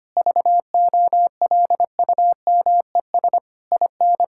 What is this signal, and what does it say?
Morse code, heard as a keyed tone, about 25 wpm, 700 Hz. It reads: VOLUMES IN
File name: practice / 25wpm / Difficult Sentences with Hints - 25wpm